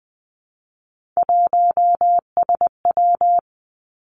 Morse code, heard as a keyed tone, about 20 wpm, 700 Hz. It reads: 1SW
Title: Three Challenging Letter-Number Combinations 20wpm